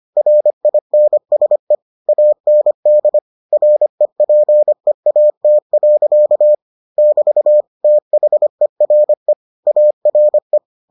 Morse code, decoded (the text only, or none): RINSE AND REPEAT. <BT> THERE ARE